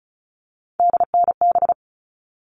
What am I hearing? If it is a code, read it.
Morse code, heard as a keyed tone, about 34 wpm, 700 Hz. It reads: DNB
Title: All Cut Numbers - 3-Digit 35wpm